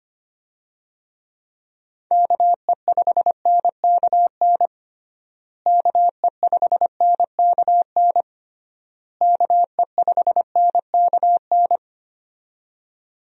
Morse code, heard as a keyed tone, about 25 wpm, 700 Hz. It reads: KE5NKN KE5NKN KE5NKN
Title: USA Call Signs - 6 Characters - Repeated 3x - 25wpm